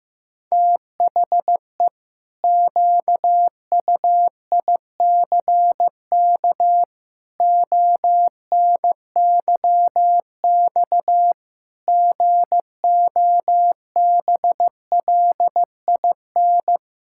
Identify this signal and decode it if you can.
Morse code, heard as a keyed tone, about 16 wpm, 700 Hz. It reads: THE QUICK ONYX GOBLIN
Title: Pangram Challenge v2 - 15wpm